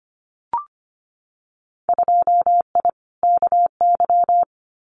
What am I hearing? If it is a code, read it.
Morse code, heard as a keyed tone, about 25 wpm, 700 Hz. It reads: 2I KY
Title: Winter Field Day Exchanges 25wpm